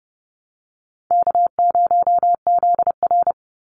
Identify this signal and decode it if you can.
Morse code, heard as a keyed tone, about 30 wpm, 700 Hz. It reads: K0ZR
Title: Easy Call Signs Repeated 3x - 30wpm